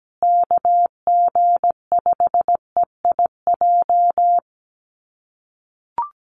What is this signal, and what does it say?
Morse code, heard as a keyed tone, about 17 wpm, 700 Hz. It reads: KG5EIJ
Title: USA Call Signs Repeated 3x 17wpm